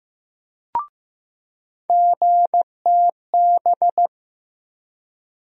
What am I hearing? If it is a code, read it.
Morse code, heard as a keyed tone, about 15 wpm, 700 Hz. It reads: GTB
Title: USA Call Sign Postfixes - 3 Letters 15wpm